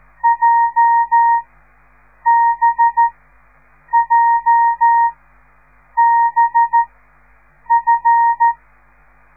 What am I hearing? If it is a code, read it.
Morse code, heard as a keyed tone, about 13 wpm, spaced as for 7 wpm, 930 Hz. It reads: JBJBF